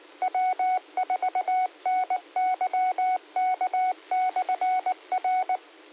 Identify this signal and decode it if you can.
Morse code, heard as a keyed tone, about 19 wpm, 740 Hz. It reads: W4NYK/R